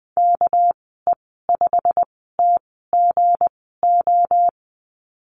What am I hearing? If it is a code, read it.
Morse code, heard as a keyed tone, about 20 wpm, spaced as for 15 wpm, 700 Hz. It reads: KE5TGO
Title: 219 Lesson 35g - Number 0 - Call Signs 20wpm